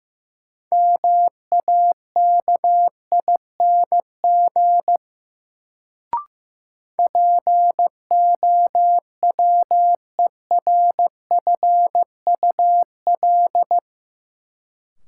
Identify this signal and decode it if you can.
Morse code, heard as a keyed tone, about 15 wpm, 700 Hz. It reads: MAKING POWERFUL
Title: Sentences from Top 1000 Words - Review Part 6 - New Word Drill - 15wpm